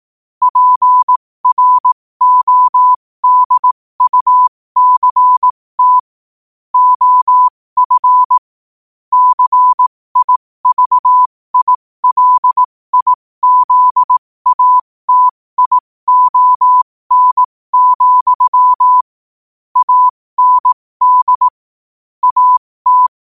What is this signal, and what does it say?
Morse code, heard as a keyed tone, about 18 wpm, spaced as for 15 wpm, 1000 Hz. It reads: PRODUCT OF CIVILIZATION, AND AT
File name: qotdfiles / 15wpm